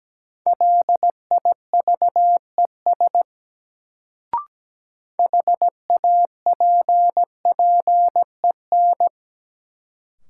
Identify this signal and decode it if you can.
Morse code, heard as a keyed tone, about 17 wpm, 700 Hz. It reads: LIVES HAPPEN
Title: Sets of 5 Words Encore - New Word Drill - 17wpm